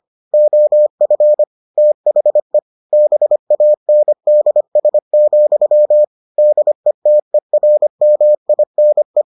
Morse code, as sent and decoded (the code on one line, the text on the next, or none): --- ..-. / - .... / -... .- -. -.. ... --..-- / -.. - . .-. -- .. -. .
OF THE BANDS, DETERMINE